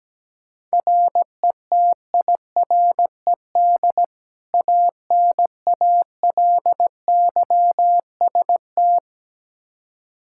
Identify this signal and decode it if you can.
Morse code, heard as a keyed tone, about 17 wpm, 700 Hz. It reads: RETIRED ANALYST
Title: Practice QSO - Exchange 3 - More Info - 17wpm